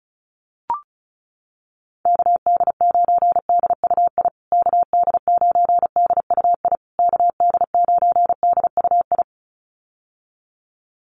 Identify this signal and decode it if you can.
Morse code, heard as a keyed tone, about 34 wpm, 700 Hz. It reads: KD9DUI KD9DUI KD9DUI
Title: USA Call Signs Repeated 3x 35wpm